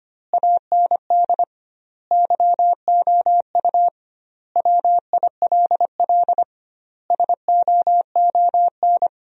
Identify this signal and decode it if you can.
Morse code, heard as a keyed tone, about 25 wpm, 700 Hz. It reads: AND YOU WILL SOON
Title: 2x Word-Spacing - Sentences from Top 400 Words - 25wpm